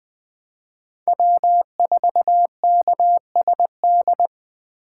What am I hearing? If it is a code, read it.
Morse code, heard as a keyed tone, about 20 wpm, 700 Hz. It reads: W4KSD
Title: Speed Racing - USA 1x3 Call Signs 17wpm